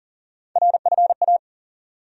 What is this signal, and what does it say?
Morse code, heard as a keyed tone, about 40 wpm, 700 Hz. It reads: RFA
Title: USA Call Sign Postfixes - 3 Letters 40wpm